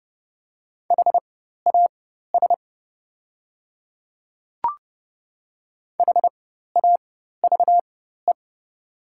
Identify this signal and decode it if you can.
Morse code, heard as a keyed tone, about 30 wpm, spaced as for 14 wpm, 700 Hz. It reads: HAS HAVE
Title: 4x Character Spacing - Top 200 Words - 30wpm